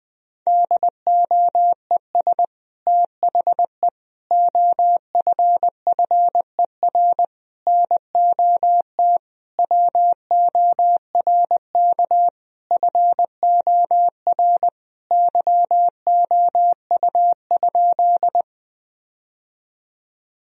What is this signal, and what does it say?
Morse code, heard as a keyed tone, about 21 wpm, 700 Hz. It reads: DOES THE OFFER NOT WORK FOR YOU?
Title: Sentences from Top 1000 Words - Review Part 4 - 20wpm